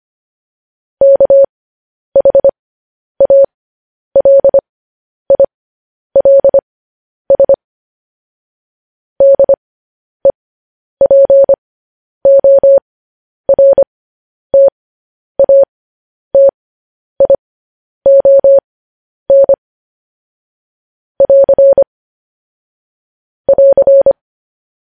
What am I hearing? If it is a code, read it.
Morse code, heard as a keyed tone, about 25 wpm, spaced as for 10 wpm, 550 Hz. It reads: KHALILS DEPORTATION <AR> <AR>
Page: news headlines in morse code at 10 wpm